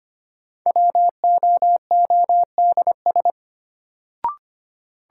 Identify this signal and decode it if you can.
Morse code, heard as a keyed tone, about 25 wpm, 700 Hz. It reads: WOODS
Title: Sentences from Top 1000 Words - Review Part 2 - New Word Drill - 25wpm